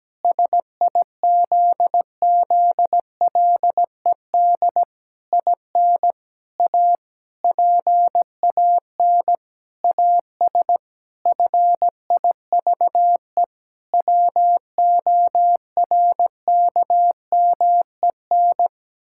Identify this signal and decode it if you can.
Morse code, heard as a keyed tone, about 17 wpm, 700 Hz. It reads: SIZZLED IN A PAN AS FIVE WORKMEN
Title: Pangram Challenge v2 - 17wpm